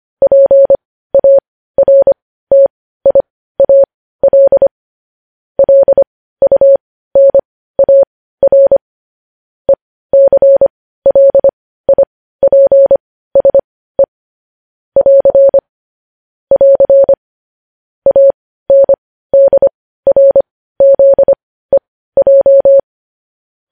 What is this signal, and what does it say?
Morse code, heard as a keyed tone, about 25 wpm, spaced as for 15 wpm, 550 Hz. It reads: PARTIAL LUNAR ECLIPSE <AR> <AR> ANDRZEJ
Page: news headlines in morse code at 15 wpm